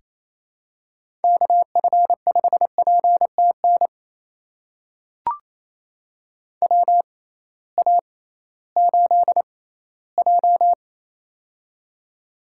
Morse code, heard as a keyed tone, about 28 wpm, spaced as for 16 wpm, 700 Hz. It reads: KF5PTN WA8J
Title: USA Call Signs - 6x Character Spacing - 28wpm